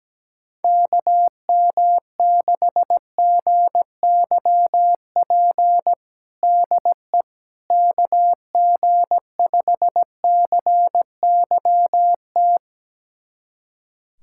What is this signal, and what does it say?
Morse code, heard as a keyed tone, about 17 wpm, 700 Hz. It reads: KM6GYP DE KG5CYT
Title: USA Callsign DE Callsign - 6 Characters - 17wpm